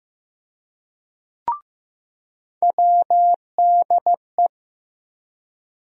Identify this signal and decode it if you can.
Morse code, heard as a keyed tone, about 15 wpm, 700 Hz. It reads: WDE
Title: USA Call Sign Postfixes - 3 Letters 15wpm